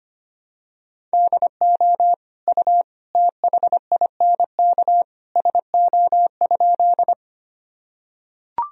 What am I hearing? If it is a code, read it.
Morse code, heard as a keyed tone, about 25 wpm, 700 Hz. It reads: DO U THINK SO?
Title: Sentences from Top 200 Words and Abbreviations 25wpm